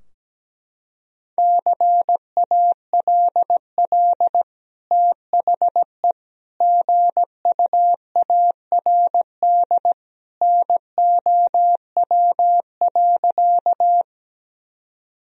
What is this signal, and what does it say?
Morse code, heard as a keyed tone, about 17 wpm, 700 Hz. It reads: CALL THE GUARD NOW.
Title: Sentences from Top 1000 Words - Review Part 4 - 17wpm